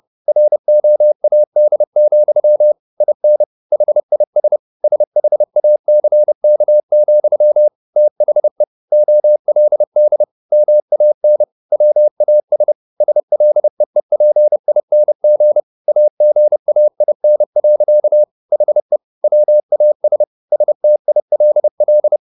Morse code, heard as a keyed tone, about 30 wpm, 600 Hz. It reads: ROAD, IN HIS SHACK, THE OLD MAN WAS SLEEPING AGAIN. HE WAS STILL